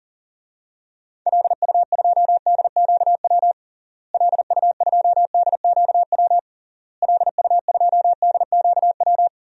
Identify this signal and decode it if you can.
Morse code, heard as a keyed tone, about 40 wpm, 700 Hz. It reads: LU2DQW LU2DQW LU2DQW
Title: DXCC-100 Call Signs - 6 Characters - Repeated 3x - 40wpm